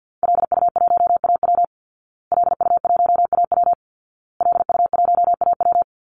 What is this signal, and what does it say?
Morse code, heard as a keyed tone, about 50 wpm, 700 Hz. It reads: LU1AW LU1AW LU1AW
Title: Call Signs Repeated 2x - 50wpm